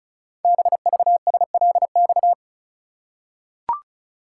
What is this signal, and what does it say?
Morse code, heard as a keyed tone, about 36 wpm, 700 Hz. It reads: BVSLX
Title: Code Groups - 5 Letter:Numbers - 35wpm